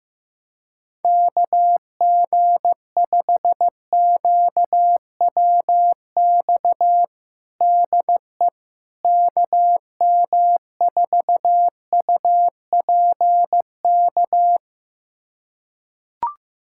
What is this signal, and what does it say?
Morse code, heard as a keyed tone, about 15 wpm, 700 Hz. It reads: KG5QWX DE KM4UPK
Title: USA Callsign DE Callsign - 6 Characters - 15wpm